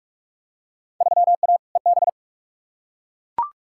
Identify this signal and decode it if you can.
Morse code, heard as a keyed tone, about 41 wpm, 700 Hz. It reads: 3A EB
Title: Field Day Exchanges Encore - 45wpm